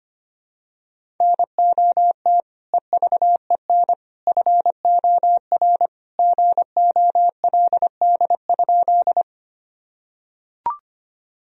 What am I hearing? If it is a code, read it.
Morse code, heard as a keyed tone, about 25 wpm, 700 Hz. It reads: NOT EVEN FOR GOLD?
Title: Sentences from Top 1000 Words - Review Part 3 - 25wpm